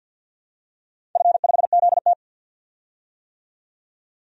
Morse code, heard as a keyed tone, about 48 wpm, 700 Hz. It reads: F5ZT
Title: DXCC-100 Call Signs - 4 Characters - 50wpm